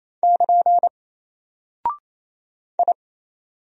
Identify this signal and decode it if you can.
Morse code, heard as a keyed tone, about 27 wpm, 700 Hz. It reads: <KN> I
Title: Single Character (Letter:Number:Punctuation:Prosign) 28wpm